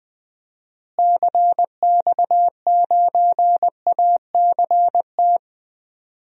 Morse code, heard as a keyed tone, about 20 wpm, 700 Hz. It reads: CX9ACT
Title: DXCC-100 Call Signs - 6 Characters - Repeated 2x - 20wpm